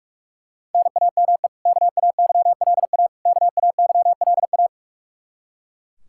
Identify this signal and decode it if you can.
Morse code, heard as a keyed tone, about 41 wpm, 700 Hz. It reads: NAME KAYLA KAYLA
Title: Practice QSO - Exchange 1 - Beginning - 45wpm